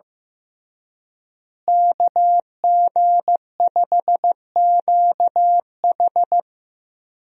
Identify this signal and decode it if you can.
Morse code, heard as a keyed tone, about 15 wpm, 700 Hz. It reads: KG5QH
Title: USA Call Signs - 5 Characters - Repeated 3x - 15wpm